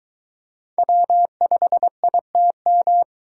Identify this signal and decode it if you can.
Morse code, heard as a keyed tone, about 22 wpm, 700 Hz. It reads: W5ITM